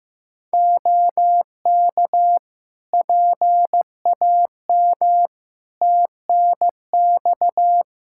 Morse code, heard as a keyed tone, about 15 wpm, 700 Hz. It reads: OK PAM TNX
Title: K1USN SST QSO Practice 15wpm